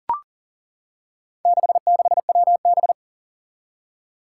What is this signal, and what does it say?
Morse code, heard as a keyed tone, about 40 wpm, 700 Hz. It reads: 66WB
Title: Four Challenging Letter-Number Combinations 40wpm